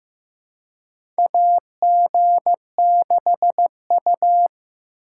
Morse code, heard as a keyed tone, about 15 wpm, 700 Hz. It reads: AG6U
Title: USA Call Signs - 4 Characters - 3x Character Spacing - 15wpm